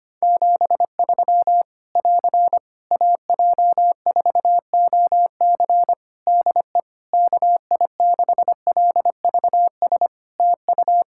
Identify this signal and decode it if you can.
Morse code, heard as a keyed tone, about 25 wpm, 700 Hz. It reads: 73 <AR> AJ4OC DE KI6LVS TU
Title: Practice QSO - Exchange 4 - Ending - 25wpm